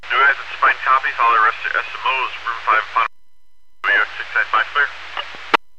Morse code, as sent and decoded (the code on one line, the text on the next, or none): none